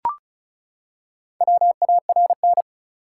Morse code, heard as a keyed tone, about 36 wpm, 700 Hz. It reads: WARN
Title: Four Letter Words - 35wpm